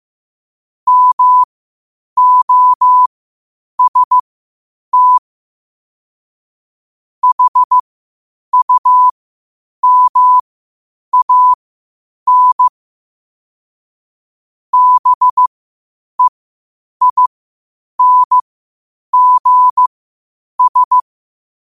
Morse code, heard as a keyed tone, about 15 wpm, spaced as for 8 wpm, 1000 Hz. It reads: MOST HUMAN BEINGS
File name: qotdfiles / 8wpm